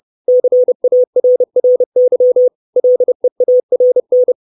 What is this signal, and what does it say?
Morse code, heard as a keyed tone, about 30 wpm, 500 Hz. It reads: CARRY LEARN
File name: 632 words / 632 Words 30WPM